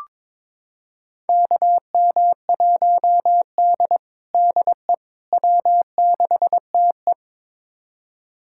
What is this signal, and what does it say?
Morse code, heard as a keyed tone, about 22 wpm, 700 Hz. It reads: KM1D DE W6TE
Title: USA Callsign DE Callsign - 4 Characters - 22wpm